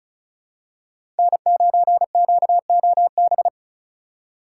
Morse code, heard as a keyed tone, about 36 wpm, 700 Hz. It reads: N9QOB
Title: USA Call Signs - 2x Character Spacing - 35wpm